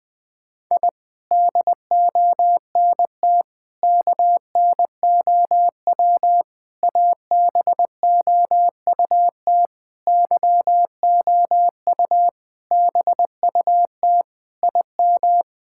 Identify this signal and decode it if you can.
Morse code, heard as a keyed tone, about 20 wpm, 700 Hz. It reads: I DONT KNOW ABOUT YOU BUT IM
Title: Sentences with 6-Word Phrases - Spoken Beginning - 20wpm